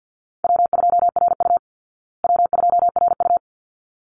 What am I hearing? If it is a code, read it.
Morse code, heard as a keyed tone, about 48 wpm, 700 Hz. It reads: W2RU W2RU
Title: Call Signs Repeated 3x - 50wpm